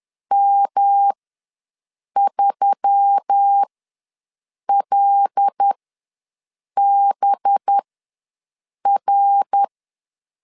Morse code, heard as a keyed tone, about 10 wpm, spaced as for 6 wpm, 800 Hz. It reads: M3LBR
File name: LEZIONI / 5 LBVC / 5 LB ws 10wpm 3616